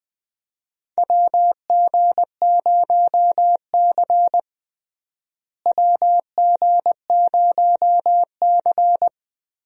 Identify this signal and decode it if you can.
Morse code, heard as a keyed tone, about 20 wpm, 700 Hz. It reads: WG0C WG0C
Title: USA Call Signs - 4 Characters - Repeated 2x - 20wpm